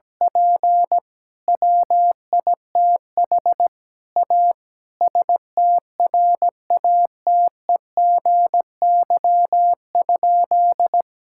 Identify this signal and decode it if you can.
Morse code, heard as a keyed tone, about 17 wpm, 700 Hz. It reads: P WITH A STRATEGY?
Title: Sentences from Top 1000 Words - Review Part 5 - 17wpm